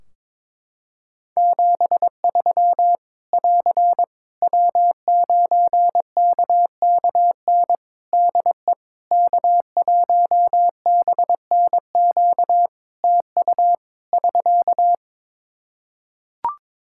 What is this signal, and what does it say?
Morse code, heard as a keyed tone, about 22 wpm, 700 Hz. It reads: 73 <AR> W9KKN DE K1BNQ TU <SK>
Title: K1USN SST QSO Practice 22wpm